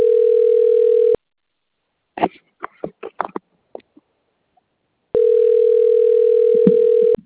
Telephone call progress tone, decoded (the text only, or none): ringback tone